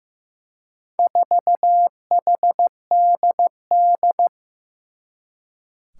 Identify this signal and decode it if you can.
Morse code, heard as a keyed tone, about 15 wpm, 700 Hz. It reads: 4HDD
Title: Four Challenging Letter-Number Combinations 15wpm